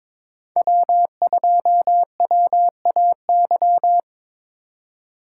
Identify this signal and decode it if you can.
Morse code, heard as a keyed tone, about 23 wpm, 700 Hz. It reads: W2WAY
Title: USA Call Signs - 5 Characters - 8x Character Spacing - 22wpm